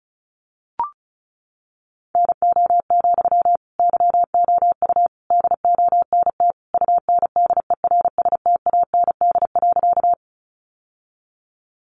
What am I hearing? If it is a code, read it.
Morse code, heard as a keyed tone, about 34 wpm, 700 Hz. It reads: NO, YOU DONT UNDERSTAND.
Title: Sentences from Top 600 Words 35wpm